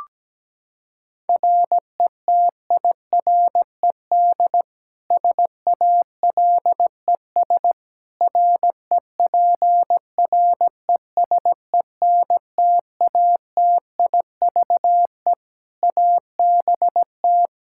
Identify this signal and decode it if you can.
Morse code, heard as a keyed tone, about 17 wpm, 700 Hz. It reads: RETIRED SALES REPRESENTATIVE ABT
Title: Practice QSO - Exchange 3 - More Info - 17wpm